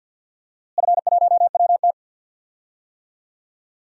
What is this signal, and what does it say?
Morse code, heard as a keyed tone, about 47 wpm, 700 Hz. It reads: F1WT
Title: DXCC-100 Call Signs - 4 Characters - Repeated 3x - 50wpm